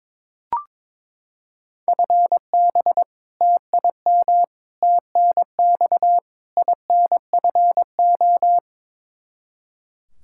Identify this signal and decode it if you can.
Morse code, heard as a keyed tone, about 22 wpm, 700 Hz. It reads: FB TIM TNX INFO
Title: Practice QSO - Exchange 3 - More Info - 22wpm